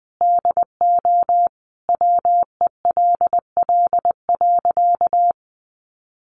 Morse code, heard as a keyed tone, about 20 wpm, 700 Hz. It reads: DO WELL.
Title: Sentences from Top 1400 Words 20wpm